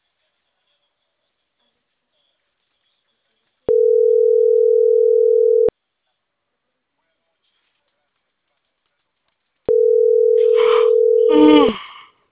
Ringback tone (North America), 440 + 480 Hz.